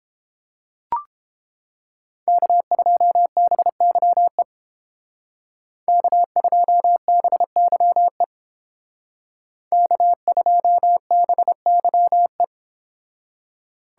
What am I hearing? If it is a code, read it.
Morse code, heard as a keyed tone, about 30 wpm, 700 Hz. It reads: K2BYE K2BYE K2BYE